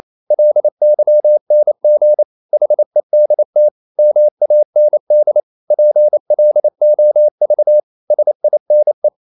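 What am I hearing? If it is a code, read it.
Morse code, heard as a keyed tone, about 28 wpm, 600 Hz. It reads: LYNG HEDT MAND PLOV SINE
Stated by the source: Danish short words sent in Morse at 28 wpm, from CW med Gnister.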